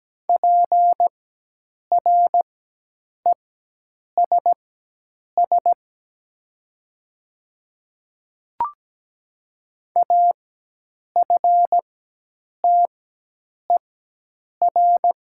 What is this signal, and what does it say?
Morse code, heard as a keyed tone, about 17 wpm, spaced as for 8 wpm, 700 Hz. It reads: PRESS AFTER
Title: 4x Character Spacing - Top 300 Words - 17wpm